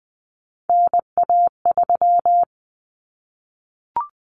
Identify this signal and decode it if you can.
Morse code, heard as a keyed tone, about 20 wpm, 700 Hz. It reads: NA3
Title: Common Cut Numbers - 3-Digit 20wpm